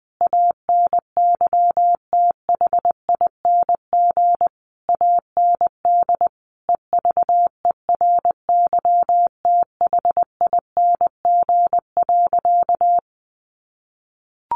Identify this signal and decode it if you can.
Morse code, heard as a keyed tone, about 20 wpm, 700 Hz. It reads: ANYTHING AND EVERYTHING.
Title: Sentences from Top 1200 Words 20wpm